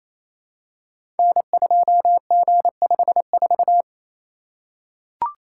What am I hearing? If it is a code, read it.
Morse code, heard as a keyed tone, about 27 wpm, 700 Hz. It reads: N2G54
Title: Code Groups - 5 Letter:Numbers - 28wpm